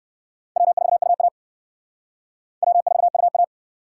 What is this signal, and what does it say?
Morse code, heard as a keyed tone, about 57 wpm, 700 Hz. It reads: W4UA W4UA